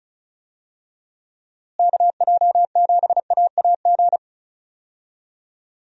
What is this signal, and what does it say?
Morse code, heard as a keyed tone, about 34 wpm, 700 Hz. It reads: KJ7AAG
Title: USA Call Signs - 6 Characters - Repeated 2x - 35wpm